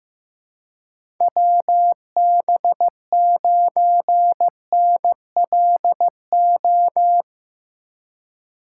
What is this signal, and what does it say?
Morse code, heard as a keyed tone, about 15 wpm, 700 Hz. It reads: WB9NLO
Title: USA Call Signs - 6 Characters - 4x Character Spacing - 15wpm